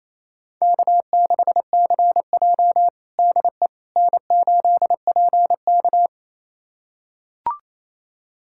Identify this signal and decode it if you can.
Morse code, heard as a keyed tone, about 28 wpm, 700 Hz. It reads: K6CJ DE N8PK
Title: USA Callsign DE Callsign - 4 Characters - 28wpm